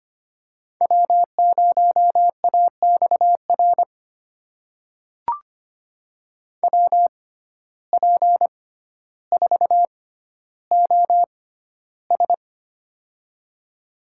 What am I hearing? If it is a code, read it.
Morse code, heard as a keyed tone, about 25 wpm, spaced as for 13 wpm, 700 Hz. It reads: W0AXR WP4OS
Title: USA Call Signs - 5 Characters - 6x Character Spacing - 25wpm